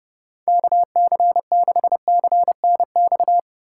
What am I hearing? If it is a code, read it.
Morse code, heard as a keyed tone, about 30 wpm, 700 Hz. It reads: KC6CNX
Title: USA Call Signs - 6 Characters - 2x Character Spacing - 30wpm